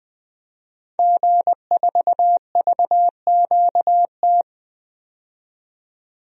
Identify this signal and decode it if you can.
Morse code, heard as a keyed tone, about 20 wpm, 700 Hz. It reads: G4VQT
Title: DXCC-100 Call Signs - 5 Characters - 20wpm